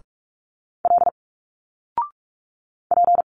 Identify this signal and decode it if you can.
Morse code, heard as a keyed tone, about 46 wpm, 700 Hz. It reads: L P
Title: Single Letters 45wpm